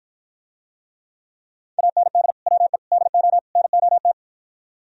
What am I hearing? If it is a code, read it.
Morse code, heard as a keyed tone, about 52 wpm, 700 Hz. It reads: AND WE DO NOT